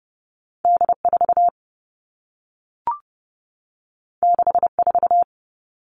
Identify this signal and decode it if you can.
Morse code, heard as a keyed tone, about 30 wpm, 700 Hz. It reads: D 4 6 4
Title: Two Challenging Letter-Number Combinations 30wpm